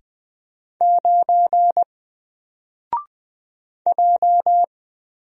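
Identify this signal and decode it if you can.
Morse code, heard as a keyed tone, about 20 wpm, 700 Hz. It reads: 9 J
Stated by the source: Single letter and number Morse drill at 20 wpm, from N4LCD.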